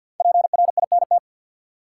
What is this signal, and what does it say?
Morse code, heard as a keyed tone, about 52 wpm, 700 Hz. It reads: PRINT
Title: Sentences from Top 1400 Words - New Word Drill - 50wpm